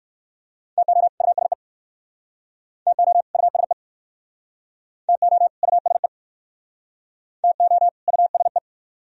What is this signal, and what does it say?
Morse code, heard as a keyed tone, about 61 wpm, 690 Hz. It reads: TO USE TO USE TO USE TO USE